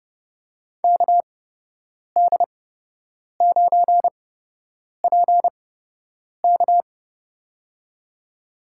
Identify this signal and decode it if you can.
Morse code, heard as a keyed tone, about 31 wpm, spaced as for 8 wpm, 700 Hz. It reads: KD9PK
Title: USA Call Signs - 5 Characters - 8x Character Spacing - 30wpm